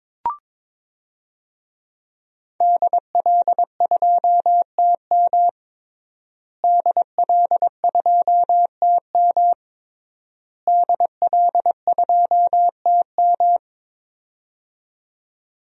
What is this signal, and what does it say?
Morse code, heard as a keyed tone, about 22 wpm, 700 Hz. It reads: DL2TM DL2TM DL2TM
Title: DXCC-100 Call Signs - 5 Characters - Repeated 3x - 22wpm